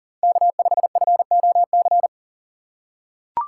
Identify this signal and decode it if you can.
Morse code, heard as a keyed tone, about 39 wpm, 700 Hz. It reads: K5FOC
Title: USA Call Signs - 5 Characters - 6x Character Spacing - 40wpm